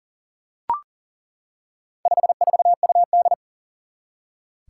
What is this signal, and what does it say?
Morse code, heard as a keyed tone, about 38 wpm, 700 Hz. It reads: 54UD
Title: Four Challenging Letter-Number Combinations 40wpm